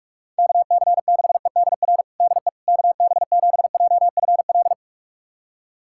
Morse code, heard as a keyed tone, about 42 wpm, 700 Hz. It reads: KC6EDR DE KB7JFL
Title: USA Callsign DE Callsign 45wpm